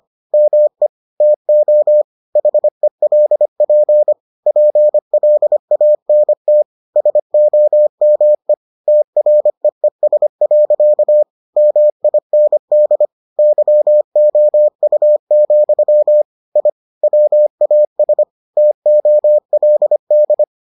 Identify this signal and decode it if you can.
Morse code, heard as a keyed tone, about 25 wpm, 600 Hz. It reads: ME TO HELP PLANT SOME TREES. MIND YOU, I WAS TOLD